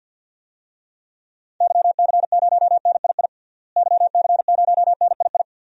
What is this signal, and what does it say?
Morse code, heard as a keyed tone, about 50 wpm, 700 Hz. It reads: YC0NII YC0NII
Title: DXCC-100 Call Signs - 6 Characters - Repeated 2x - 50wpm